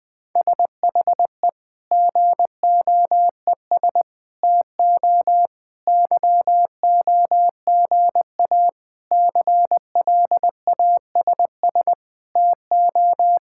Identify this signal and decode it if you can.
Morse code, heard as a keyed tone, about 20 wpm, 700 Hz. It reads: SHE GOES TO YOGA CLASS TO
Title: Difficult Sentences with Hints - 20wpm